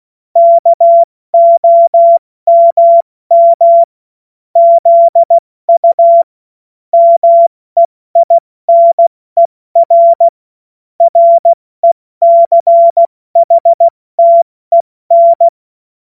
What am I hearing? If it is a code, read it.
Morse code, heard as a keyed tone, about 16 wpm, 680 Hz. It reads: KOMM ZU MEINER RECHTEN
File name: Morsekurs / Witze / Witz 09. Februar 2021, 70 F80